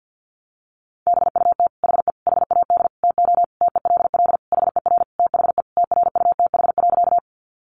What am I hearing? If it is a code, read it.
Morse code, heard as a keyed tone, about 50 wpm, 700 Hz. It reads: BUT HE HAD TO TELL HER THE TRUTH.
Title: Sentences from Top 800 Words 50wpm